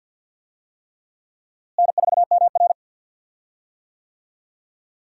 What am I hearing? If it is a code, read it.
Morse code, heard as a keyed tone, about 52 wpm, 700 Hz. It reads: N4MR
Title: USA Call Signs - 4 Characters - Repeated 3x - 50wpm